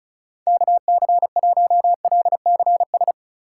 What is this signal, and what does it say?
Morse code, heard as a keyed tone, about 36 wpm, 700 Hz. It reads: KC1LCS
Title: USA Call Signs - 6 Characters - 2x Character Spacing - 35wpm